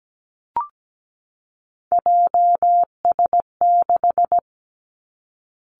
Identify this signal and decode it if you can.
Morse code, heard as a keyed tone, about 17 wpm, 700 Hz. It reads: JS6
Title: Three Challenging Letter-Number Combinations 17wpm